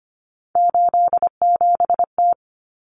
Morse code, heard as a keyed tone, about 25 wpm, 700 Hz. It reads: 87T
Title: Common Cut Numbers - 3-Digit 25wpm